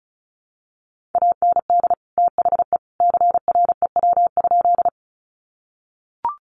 Morse code, heard as a keyed tone, about 34 wpm, 700 Hz. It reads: AND THE CREW?
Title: Sentences from Top 1500 Words 35wpm